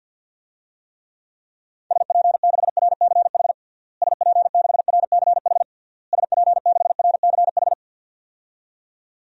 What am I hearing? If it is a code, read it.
Morse code, heard as a keyed tone, about 53 wpm, 700 Hz. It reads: SP6RCH SP6RCH SP6RCH
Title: DXCC-100 Call Signs - 6 Characters - Repeated 3x - 50wpm